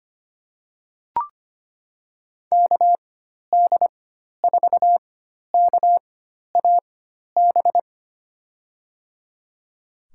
Morse code, heard as a keyed tone, about 25 wpm, spaced as for 12 wpm, 700 Hz. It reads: KD4KAB
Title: USA Call Signs - 6 Characters - 4x Character Spacing - 25wpm